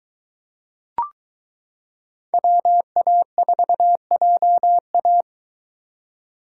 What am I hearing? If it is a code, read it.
Morse code, heard as a keyed tone, about 22 wpm, 700 Hz. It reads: WA4JA